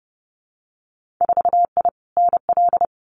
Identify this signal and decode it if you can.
Morse code, heard as a keyed tone, about 30 wpm, 700 Hz. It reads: 4I NL
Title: Winter Field Day Exchanges Encore 30wpm